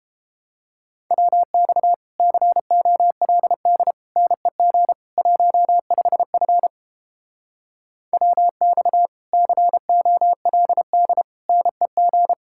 Morse code, heard as a keyed tone, about 29 wpm, 700 Hz. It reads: WX COLD NEG 15F WX COLD NEG